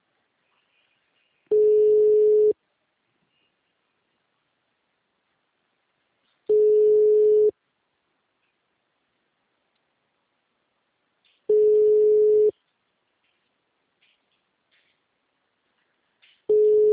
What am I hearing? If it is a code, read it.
Ringback tone (Europe, 425 Hz).
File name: out- -2051- - - .85520